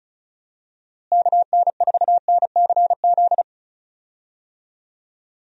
Morse code, heard as a keyed tone, about 34 wpm, 700 Hz. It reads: KN4NCZ
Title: USA Call Signs - 6 Characters - Repeated 3x - 35wpm